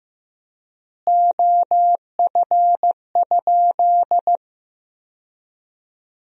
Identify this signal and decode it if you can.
Morse code, heard as a keyed tone, about 15 wpm, 700 Hz. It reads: OF?
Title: 4x Word-Spacing - Sentences from Top 200 Words - 15wpm